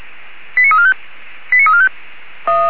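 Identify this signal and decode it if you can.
Five-tone selective call (ZVEI), heard as quick runs of five tones, tones of about 70 ms. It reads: 89345 89345
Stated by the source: Received on SDR, ZVEI five-tone selective call 89352.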